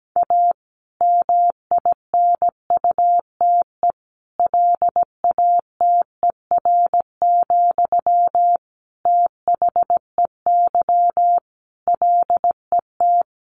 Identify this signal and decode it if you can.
Morse code, heard as a keyed tone, about 17 wpm, 700 Hz. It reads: A MINUTE LATER, THEY LET
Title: Sentences from Top 1500 Words 17wpm